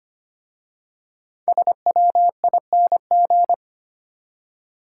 Morse code, heard as a keyed tone, about 25 wpm, 700 Hz. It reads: SWING
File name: practice / 25wpm / Easy Words with Hints - 25wpm